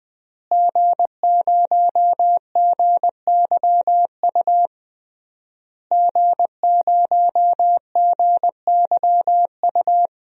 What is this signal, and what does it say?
Morse code, heard as a keyed tone, about 20 wpm, 700 Hz. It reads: G0GYU G0GYU
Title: DXCC-100 Call Signs - 5 Characters - Repeated 3x - 20wpm